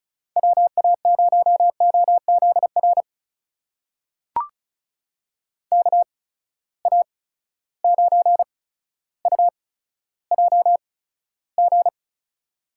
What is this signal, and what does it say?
Morse code, heard as a keyed tone, about 34 wpm, spaced as for 15 wpm, 700 Hz. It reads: WA0OZR KA9UJG
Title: USA Call Signs - 6 Characters - 8x Character Spacing - 35wpm